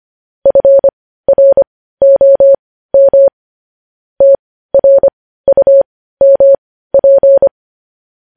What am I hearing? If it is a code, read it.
Morse code, heard as a keyed tone, about 25 wpm, spaced as for 15 wpm, 550 Hz. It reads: FROM TRUMP
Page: News headlines in morse code at 15 wpm